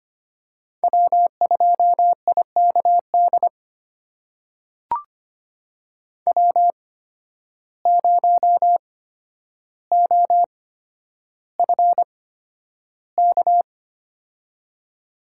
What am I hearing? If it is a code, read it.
Morse code, heard as a keyed tone, about 25 wpm, spaced as for 11 wpm, 700 Hz. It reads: W2IKD W0OFK
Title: USA Call Signs - 5 Characters - 8x Character Spacing - 25wpm